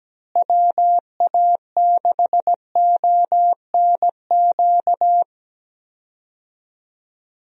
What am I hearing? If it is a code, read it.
Morse code, heard as a keyed tone, about 17 wpm, 700 Hz. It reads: WA6ONQ
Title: USA Call Signs - 6 Characters - Repeated 2x - 17wpm